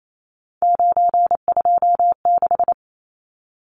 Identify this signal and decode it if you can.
Morse code, heard as a keyed tone, about 28 wpm, 700 Hz. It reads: 926
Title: Numbers - 3-Digit 28wpm